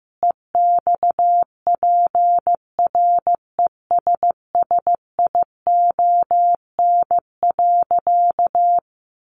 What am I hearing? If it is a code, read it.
Morse code, heard as a keyed tone, about 15 wpm, 700 Hz. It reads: EXPRESSION.
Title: Sentences from Top 1200 Words 15wpm